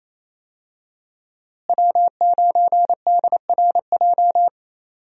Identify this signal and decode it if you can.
Morse code, heard as a keyed tone, about 27 wpm, 700 Hz. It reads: W9DRJ
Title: USA Call Signs - 5 Characters - Repeated 3x - 28wpm